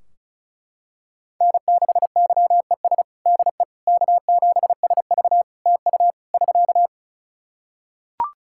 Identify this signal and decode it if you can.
Morse code, heard as a keyed tone, about 34 wpm, 700 Hz. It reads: N6YES DE K7SV TU <SK>
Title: K1USN SST QSO Practice 35wpm